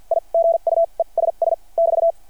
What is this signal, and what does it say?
Morse code, heard as a keyed tone, about 53 wpm, spaced as for 39 wpm, 660 Hz. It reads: I GUESS <BT>
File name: archived / CRSnet